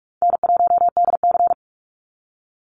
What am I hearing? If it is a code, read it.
Morse code, heard as a keyed tone, about 46 wpm, 700 Hz. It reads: N1DC
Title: Call Signs - 45wpm